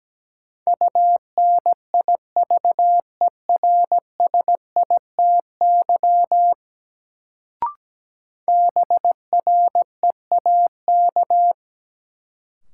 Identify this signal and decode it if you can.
Morse code, heard as a keyed tone, about 17 wpm, 700 Hz. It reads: UNIVERSITY BREAK
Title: Sets of 4 Words - New Word Drill - 17wpm